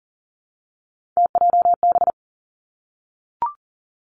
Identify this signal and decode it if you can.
Morse code, heard as a keyed tone, about 38 wpm, 700 Hz. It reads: TJB
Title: Three Challenging Letter-Number Combinations 40wpm